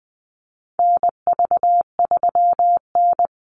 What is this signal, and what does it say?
Morse code, heard as a keyed tone, about 20 wpm, 700 Hz. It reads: NV3N
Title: Call Signs Repeated 2x - 20wpm